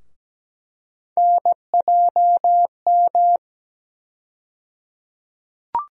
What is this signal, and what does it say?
Morse code, heard as a keyed tone, about 17 wpm, 700 Hz. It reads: NJM
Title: USA Call Sign Postfixes - 3 Letters 17wpm